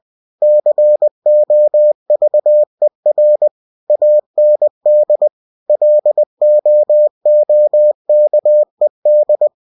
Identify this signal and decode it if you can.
Morse code, heard as a keyed tone, about 20 wpm, 600 Hz. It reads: COVER AND LOOKED
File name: cw-books / winnie-the-pooh / winnie-20wpm-page-0037